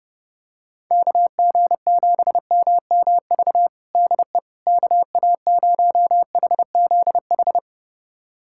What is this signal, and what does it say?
Morse code, heard as a keyed tone, about 30 wpm, 700 Hz. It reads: KG7MMV DE KA0HZH
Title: USA Callsign DE Callsign 30wpm